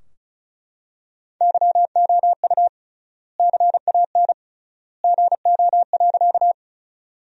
Morse code, heard as a keyed tone, about 36 wpm, 700 Hz. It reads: YOU CAN GO.
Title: 3x Word-Spacing - Sentences from Top 100 Words - 35wpm